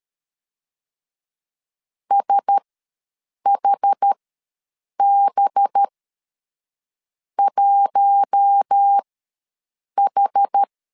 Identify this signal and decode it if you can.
Morse code, heard as a keyed tone, about 13 wpm, spaced as for 7 wpm, 800 Hz. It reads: SHB 1H
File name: LEZIONI / 5 LBVC / 5 LBVC ws 12wpm 3159